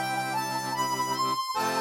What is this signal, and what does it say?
There is no code.